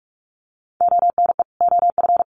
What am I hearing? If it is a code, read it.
Morse code, heard as a keyed tone, about 44 wpm, 700 Hz. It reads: ONE OF
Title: Sets of 2 Words 45wpm